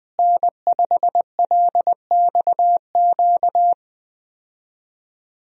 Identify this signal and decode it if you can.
Morse code, heard as a keyed tone, about 20 wpm, 700 Hz. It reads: N5LXQ
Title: USA Call Signs - 5 Characters - Repeated 2x - 20wpm